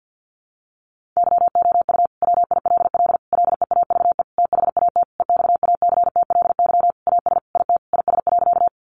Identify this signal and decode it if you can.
Morse code, heard as a keyed tone, about 50 wpm, 700 Hz. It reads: YOU WILL LEAVE THAT EXACTLY AS IT IS.
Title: Sentences from Top 900 Words 50wpm